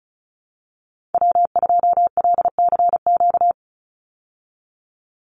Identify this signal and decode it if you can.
Morse code, heard as a keyed tone, about 34 wpm, 700 Hz. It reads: W2LCQ
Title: Call Signs Repeated 3x - 35wpm